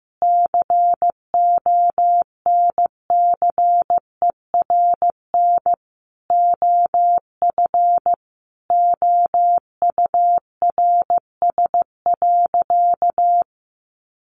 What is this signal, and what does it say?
Morse code, heard as a keyed tone, about 15 wpm, 700 Hz. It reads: CONCERN OF OURS.
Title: Sentences from Top 1300 Words 15wpm